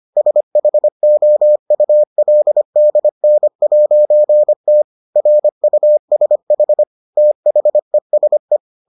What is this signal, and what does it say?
Morse code, heard as a keyed tone, about 25 wpm, 600 Hz. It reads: SHOULDN'T RUSH THESE